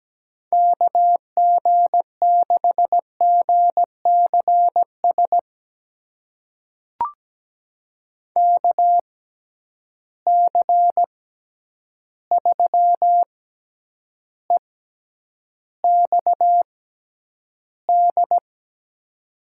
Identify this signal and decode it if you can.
Morse code, heard as a keyed tone, about 17 wpm, spaced as for 9 wpm, 700 Hz. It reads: KG6GCS KC3EXD
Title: USA Call Signs - 6 Characters - 6x Character Spacing - 17wpm